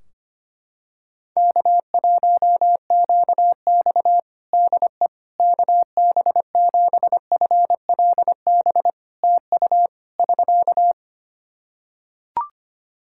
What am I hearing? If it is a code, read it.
Morse code, heard as a keyed tone, about 25 wpm, 700 Hz. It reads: K1QX DE KB7FLB TU <SK>
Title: Practice QSO - Exchange 4 - Ending - 25wpm